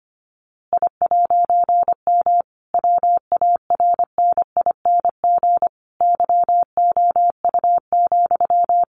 Morse code, heard as a keyed tone, about 25 wpm, 700 Hz. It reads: I'M WARNING YOU,
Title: A Sound Of Thunder 25wpm